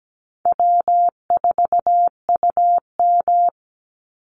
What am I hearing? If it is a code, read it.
Morse code, heard as a keyed tone, about 17 wpm, 700 Hz. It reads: W4UM
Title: Easy Call Signs Repeated 3x - 17wpm